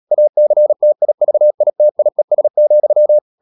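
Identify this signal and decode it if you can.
Morse code, heard as a keyed tone, about 37 wpm, 600 Hz. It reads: ACTIVITIES,
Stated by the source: Morse code practice at 35–39 wpm sending English text of Ken Follett's novel Never, from CW med Gnister.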